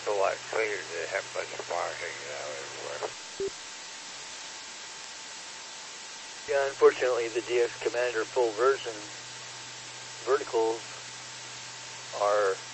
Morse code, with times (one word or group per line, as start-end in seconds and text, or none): none